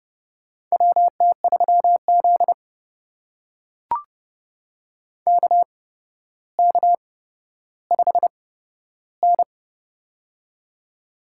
Morse code, heard as a keyed tone, about 28 wpm, spaced as for 13 wpm, 700 Hz. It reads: WT3Z KK5N
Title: USA Call Signs - 4 Characters - 8x Character Spacing - 30wpm